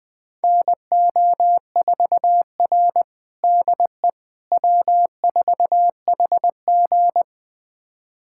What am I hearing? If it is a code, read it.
Morse code, heard as a keyed tone, about 20 wpm, 700 Hz. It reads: NO4R DE W4HG
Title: USA Callsign DE Callsign - 4 Characters - 20wpm